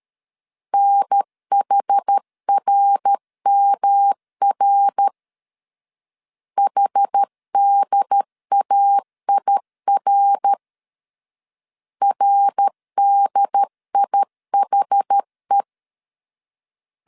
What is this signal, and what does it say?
Morse code, heard as a keyed tone, about 13 wpm, 800 Hz. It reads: NHRMR HDAIR RDIHE
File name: CW / DET / 1 3 1500C S A 12wpm 2905